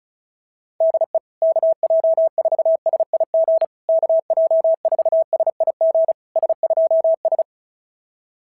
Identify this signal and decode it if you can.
Morse code, heard as a keyed tone, about 36 wpm, 650 Hz. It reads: DE KJ4SIG KJ4SIG S2S
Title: SOTA QSO Practice 35wpm